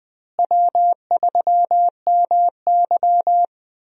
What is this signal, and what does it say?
Morse code, heard as a keyed tone, about 20 wpm, 700 Hz. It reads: W3MY
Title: USA Call Signs - 4 Characters - Repeated 3x - 20wpm